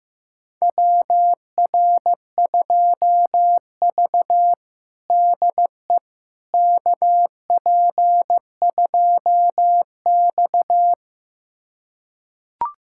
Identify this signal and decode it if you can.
Morse code, heard as a keyed tone, about 16 wpm, 700 Hz. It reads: WR2V DE KP2X
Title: USA Callsign DE Callsign - 4 Characters - 15wpm